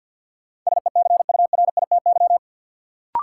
Morse code, heard as a keyed tone, about 48 wpm, 700 Hz. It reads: SECURITY
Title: Sentences from Top 1000 Words - Review Part 4 - New Word Drill - 50wpm